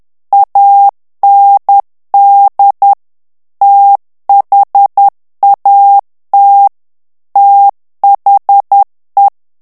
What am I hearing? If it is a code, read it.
Morse code, heard as a keyed tone, about 10 wpm, 800 Hz. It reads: AND THAT THE
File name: ANGLAIS texte clair 3 vitesse 10 mots minute